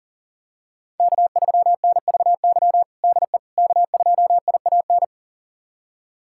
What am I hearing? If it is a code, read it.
Morse code, heard as a keyed tone, about 40 wpm, 700 Hz. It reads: K3NVY DE K2IAN
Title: USA Callsign DE Callsign - 5 Characters - 40wpm